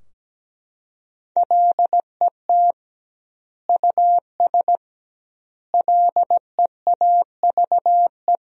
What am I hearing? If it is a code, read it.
Morse code, heard as a keyed tone, about 17 wpm, 700 Hz. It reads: LET US LEAVE
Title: 2x Word-Spacing - Sentences from Top 400 Words - 17wpm